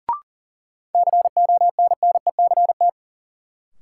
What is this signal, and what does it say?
Morse code, heard as a keyed tone, about 40 wpm, 700 Hz. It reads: CONNECT
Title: Sentences from Top 1500 Words - New Word Drill Alt - 40wpm